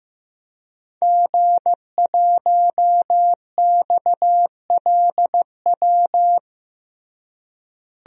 Morse code, heard as a keyed tone, about 15 wpm, 700 Hz. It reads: G1XLW
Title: DXCC-100 Call Signs - 5 Characters - Repeated 2x - 15wpm